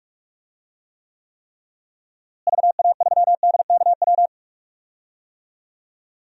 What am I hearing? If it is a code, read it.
Morse code, heard as a keyed tone, about 42 wpm, 700 Hz. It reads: VA3DKW
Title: DXCC-100 Call Signs - 6 Characters - Repeated 3x - 45wpm